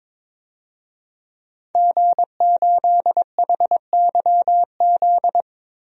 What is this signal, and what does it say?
Morse code, heard as a keyed tone, about 23 wpm, 700 Hz. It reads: G8HYZ
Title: DXCC-100 Call Signs - 5 Characters - 22wpm